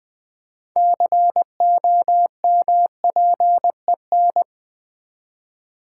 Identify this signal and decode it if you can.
Morse code, heard as a keyed tone, about 20 wpm, 700 Hz. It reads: COMPEN
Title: Build A Word - 6 Letters At A Time - 12-Letter Words - 20wpm